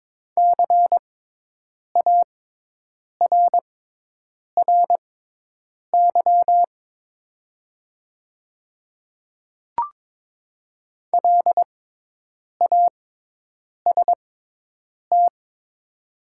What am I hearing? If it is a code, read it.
Morse code, heard as a keyed tone, about 22 wpm, spaced as for 8 wpm, 700 Hz. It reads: CARRY LAST
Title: 6x Character Spacing - Top 300 Words - 22wpm